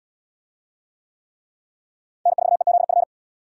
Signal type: Morse code

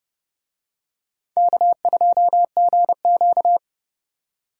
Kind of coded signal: Morse code

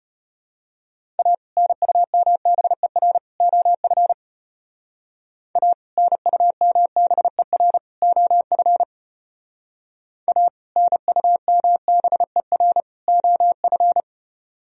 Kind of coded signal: Morse code

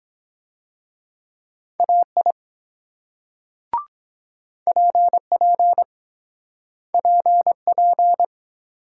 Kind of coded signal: Morse code

